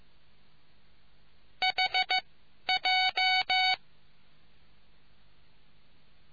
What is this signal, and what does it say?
Morse code, heard as a keyed tone, about 15 wpm, 750 Hz. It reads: H J